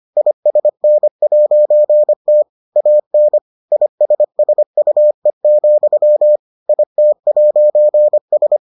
Morse code, heard as a keyed tone, about 25 wpm, 600 Hz. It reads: ISN'T AN ISSUE, IT'S